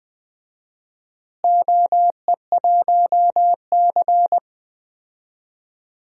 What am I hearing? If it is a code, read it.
Morse code, heard as a keyed tone, about 20 wpm, 700 Hz. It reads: OE1C
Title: DXCC-100 Call Signs - 4 Characters - Repeated 3x - 20wpm